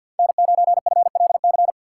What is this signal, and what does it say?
Morse code, heard as a keyed tone, about 47 wpm, 700 Hz. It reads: N9FLC
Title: USA Call Signs - 5 Characters - 6x Character Spacing - 50wpm